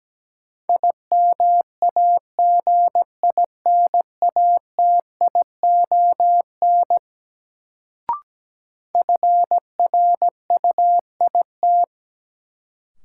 Morse code, heard as a keyed tone, about 17 wpm, 700 Hz. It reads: IMAGINATION FRUIT
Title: Sentences from Top 1300 Words - New Word Drill - 17wpm